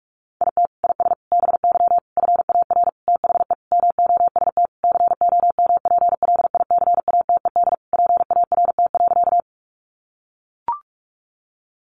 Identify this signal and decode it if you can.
Morse code, heard as a keyed tone, about 44 wpm, 700 Hz. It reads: IT IS BY FAR THE MOST COMPLICATED PART.
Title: Sentences from Top 1200 Words 45wpm